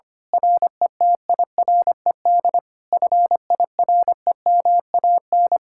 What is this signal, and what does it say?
Morse code, heard as a keyed tone, about 24 wpm, 700 Hz. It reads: RETIRED FIREMAN
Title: Speed-Racing Common QSO Phrases Encore - 25wpm